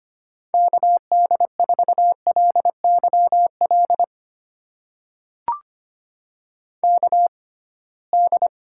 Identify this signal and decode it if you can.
Morse code, heard as a keyed tone, about 25 wpm, spaced as for 19 wpm, 700 Hz. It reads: KD4LYL KD
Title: USA Call Signs - 6 Characters - 6x Character Spacing - 25wpm